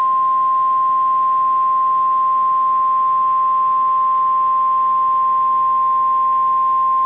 Emergency Alert System (SAME) tones (the 1050 Hz attention tone). It is heard from 0.3 to 5.9 s.
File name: EAS Logs / KMNR